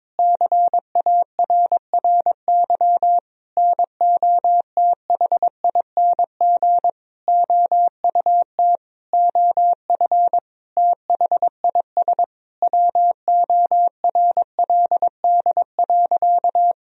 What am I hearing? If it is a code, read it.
Morse code, heard as a keyed tone, about 22 wpm, 700 Hz. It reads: CARRY NOTHING OUT OF THIS WORLD.
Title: Sentences from Top 1000 Words - Review Part 1 - 22wpm